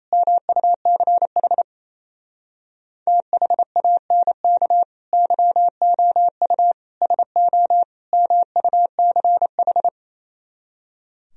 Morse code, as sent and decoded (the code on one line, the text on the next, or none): -- ..- -.-. .... / - .... .- -. -.- / -.-- --- ..- / ... --- / -- ..- -.-. ....
MUCH THANK YOU SO MUCH